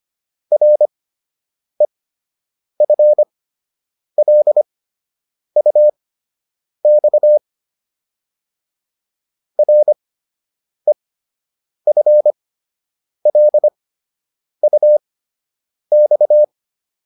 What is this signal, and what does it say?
Morse code, heard as a keyed tone, about 25 wpm, spaced as for 8 wpm, 600 Hz. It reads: REFLUX REFLUX